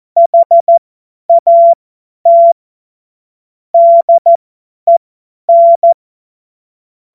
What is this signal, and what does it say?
Morse code, heard as a keyed tone, about 14 wpm, spaced as for 10 wpm, 680 Hz. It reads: HAT DEN
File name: Morsekurs / Witze / Witz 24.Oktober 2023 50 F70